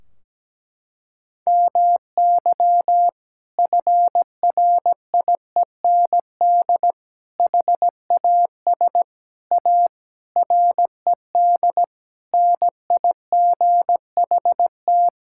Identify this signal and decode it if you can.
Morse code, heard as a keyed tone, about 17 wpm, 700 Hz. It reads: MY FRIEND HAS A RED NIGHT
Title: Sentences from Top 1600 Words 17wpm